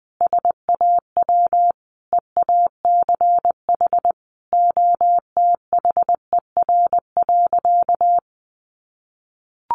Morse code, heard as a keyed tone, about 20 wpm, 700 Hz. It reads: SAW EACH OTHER.
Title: Sentences from Top 1400 Words 20wpm